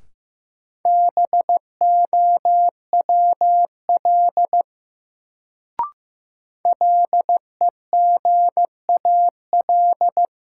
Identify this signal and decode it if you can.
Morse code, heard as a keyed tone, about 16 wpm, 700 Hz. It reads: BOWL LEGAL
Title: Sentences from Top 1200 Words - New Word Drill - 15wpm